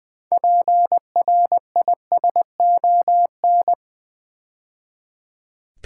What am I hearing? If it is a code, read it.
Morse code, heard as a keyed tone, about 20 wpm, 700 Hz. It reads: PRISON
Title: Build A Word - 3 Letters At A Time - 6-Letter Words - 20wpm